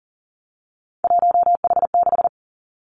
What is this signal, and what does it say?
Morse code, heard as a keyed tone, about 38 wpm, 690 Hz. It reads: 1H6
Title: Three Challenging Letter-Number Combinations 40wpm